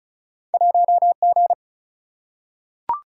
Morse code, heard as a keyed tone, about 36 wpm, 700 Hz. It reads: 1G
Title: Single-Letter Single-Number 35wpm